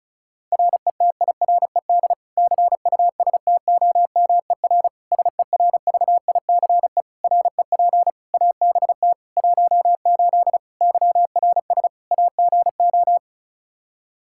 Morse code, heard as a keyed tone, about 36 wpm, 700 Hz. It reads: RETIRED CUSTOMER SERVICE REP ABT 18 YRS AGO
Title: Practice QSO - Exchange 3 - More Info - 35wpm